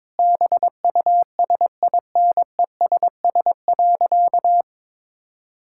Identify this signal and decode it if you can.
Morse code, heard as a keyed tone, about 22 wpm, 700 Hz. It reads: BUSINESS.
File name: practice / 22wpm / Sentences with 8-Word Phrases - 22wpm